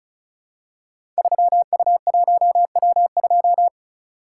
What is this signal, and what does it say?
Morse code, heard as a keyed tone, about 36 wpm, 690 Hz. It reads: 3U1W2
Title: Code Groups - 5 Letter:Numbers - 35wpm